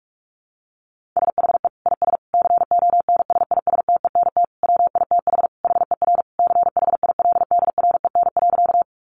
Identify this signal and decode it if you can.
Morse code, heard as a keyed tone, about 44 wpm, 700 Hz. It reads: SHE IS CONSISTENT WITH HER CHILDREN.
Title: Sentences from Top 1500 Words 45wpm